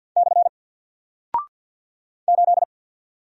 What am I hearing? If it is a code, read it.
Morse code, heard as a keyed tone, about 50 wpm, 700 Hz. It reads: / 8
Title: Single Character (Letter:Number:Punctuation:Prosign) 50wpm